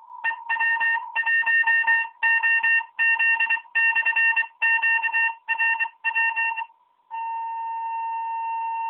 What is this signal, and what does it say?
Morse code, heard as a keyed tone, about 24 wpm, spaced as for 21 wpm, 1800 Hz, with noise. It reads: EW1OZ/QRW